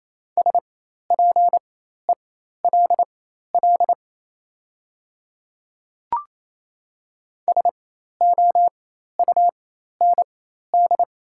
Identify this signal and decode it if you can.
Morse code, heard as a keyed tone, about 28 wpm, spaced as for 13 wpm, 700 Hz. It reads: SPELL SOUND
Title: 4x Character Spacing - Top 200 Words - 28wpm